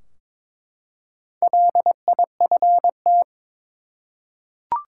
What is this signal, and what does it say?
Morse code, heard as a keyed tone, about 22 wpm, 700 Hz. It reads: LIFT
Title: Four Letter Words - 22wpm